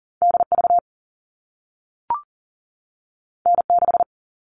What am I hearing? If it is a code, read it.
Morse code, heard as a keyed tone, about 41 wpm, 700 Hz. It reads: DV N6
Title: All Cut Numbers - 2-Digit 40wpm